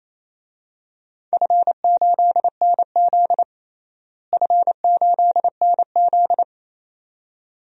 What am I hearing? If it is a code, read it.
Morse code, heard as a keyed tone, about 28 wpm, 700 Hz. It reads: F8NZ F8NZ
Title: DXCC-100 Call Signs - 4 Characters - Repeated 2x - 28wpm